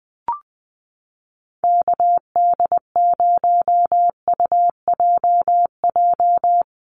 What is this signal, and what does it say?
Morse code, heard as a keyed tone, about 20 wpm, 700 Hz. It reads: KD0UJJ
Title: USA Call Signs 20wpm